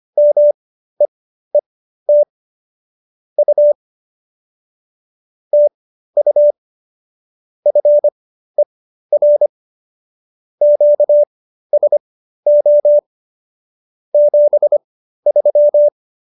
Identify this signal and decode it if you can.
Morse code, heard as a keyed tone, about 25 wpm, spaced as for 13 wpm, 590 Hz. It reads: MEET U TU FER QSO 73